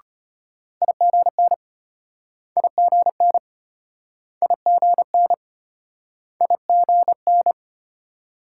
Morse code, heard as a keyed tone, about 34 wpm, 700 Hz. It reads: IGN IGN IGN IGN